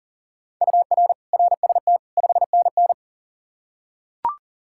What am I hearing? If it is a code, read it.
Morse code, heard as a keyed tone, about 40 wpm, 700 Hz. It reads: UR RST 5NN
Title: K1USN SST QSO Practice 40wpm